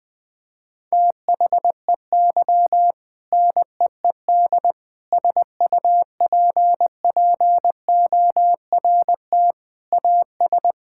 Morse code, heard as a keyed tone, about 20 wpm, 700 Hz. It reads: THEY NEED SUPPORT AS
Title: Sentences from Top 1000 Words - Review Part 5 - 20wpm